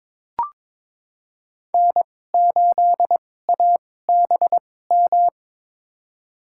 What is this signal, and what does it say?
Morse code, heard as a keyed tone, about 22 wpm, spaced as for 16 wpm, 700 Hz. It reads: N8ABM
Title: USA Call Signs - 5 Characters - 2x Character Spacing - 22wpm